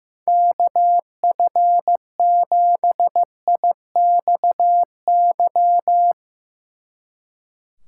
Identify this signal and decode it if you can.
Morse code, heard as a keyed tone, about 15 wpm, 700 Hz. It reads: KF7IXY
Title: Speed Racing - USA Call Signs 15wpm